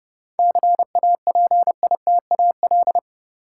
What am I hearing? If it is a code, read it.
Morse code, heard as a keyed tone, about 30 wpm, 700 Hz. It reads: CAPITAL
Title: Seven Letter Words - 30wpm